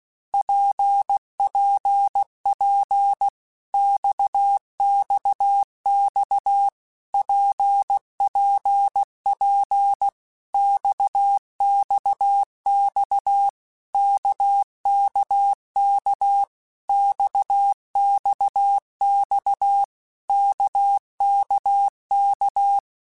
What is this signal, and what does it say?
Morse code, heard as a keyed tone, about 16 wpm, 800 Hz. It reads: PPP XXX PPP XXX KKK XXX KKK
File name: Basic Letter Groups By7Groups / RKPX / RKPX@15